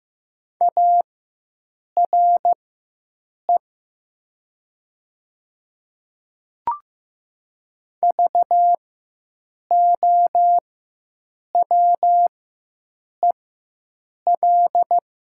Morse code, heard as a keyed tone, about 15 wpm, spaced as for 7 wpm, 700 Hz. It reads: ARE VOWEL
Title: 4x Character Spacing - Top 500 Words - 15wpm